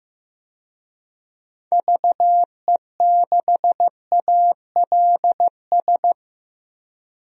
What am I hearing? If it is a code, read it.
Morse code, heard as a keyed tone, about 15 wpm, 700 Hz. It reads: VE6ALS
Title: DXCC-100 Call Signs - 6 Characters - 15wpm